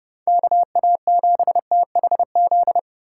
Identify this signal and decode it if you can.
Morse code, heard as a keyed tone, about 34 wpm, 700 Hz. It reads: KA7THZ